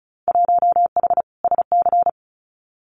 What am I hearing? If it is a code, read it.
Morse code, heard as a keyed tone, about 34 wpm, 700 Hz. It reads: 1H SC
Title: Winter Field Day Exchanges 35wpm